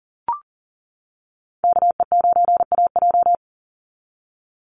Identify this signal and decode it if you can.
Morse code, heard as a keyed tone, about 40 wpm, 700 Hz. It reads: KE9AJ
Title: NA SOTA Chaser Callsigns 40wpm